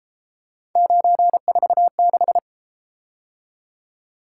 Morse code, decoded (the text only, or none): 946